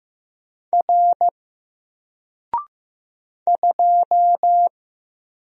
Morse code, heard as a keyed tone, about 15 wpm, 700 Hz. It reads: R 2
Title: Single Letter-Number 15wpm